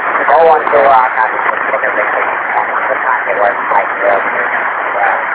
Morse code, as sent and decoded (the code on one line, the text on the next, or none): none